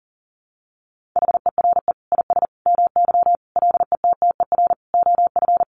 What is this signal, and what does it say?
Morse code, heard as a keyed tone, about 40 wpm, 700 Hz. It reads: HERE IS MY LETTER OF
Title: Sentences from Top 1400 Words 40wpm